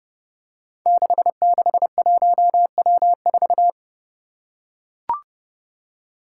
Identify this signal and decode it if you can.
Morse code, heard as a keyed tone, about 30 wpm, 700 Hz. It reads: 661W4
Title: Five Challenging Letter-Number Combinations 30wpm